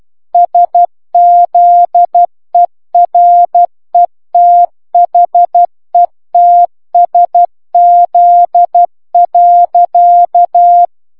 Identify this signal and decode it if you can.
Morse code, heard as a keyed tone, about 12 wpm, 700 Hz. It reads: SZERETHETSZ.